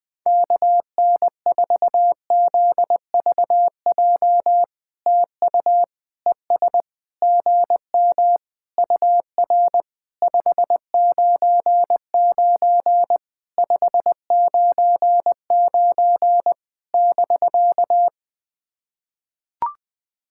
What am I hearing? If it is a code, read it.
Morse code, heard as a keyed tone, about 21 wpm, 700 Hz. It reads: KN4ZVJ TU ES GM UR 599 599 <BK>
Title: POTA QSO Practice - 20wpm